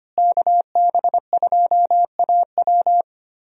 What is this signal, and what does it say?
Morse code, heard as a keyed tone, about 25 wpm, 700 Hz. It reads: KB2AW
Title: USA Call Signs - 5 Characters - Repeated 3x - 25wpm